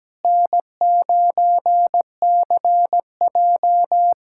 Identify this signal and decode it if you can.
Morse code, heard as a keyed tone, about 17 wpm, 700 Hz. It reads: N9CJ
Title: USA Callsign DE Callsign - 4 Characters - 17wpm